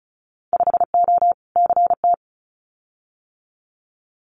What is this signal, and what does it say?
Morse code, heard as a keyed tone, about 34 wpm, 700 Hz. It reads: 5O CT
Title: Winter Field Day Exchanges Encore 35wpm